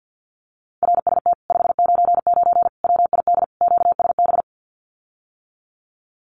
Morse code, heard as a keyed tone, about 50 wpm, 700 Hz. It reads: RST 599 WID QSB
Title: Common QSO Phrases Encore 50wpm